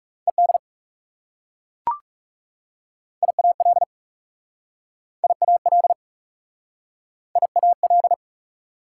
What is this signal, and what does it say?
Morse code, heard as a keyed tone, about 40 wpm, 690 Hz. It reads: ED IAL IAL IAL